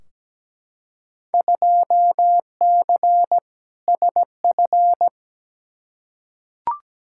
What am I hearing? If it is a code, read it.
Morse code, heard as a keyed tone, about 17 wpm, 700 Hz. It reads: 2C SF
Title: Field Day Exchanges Encore - 17wpm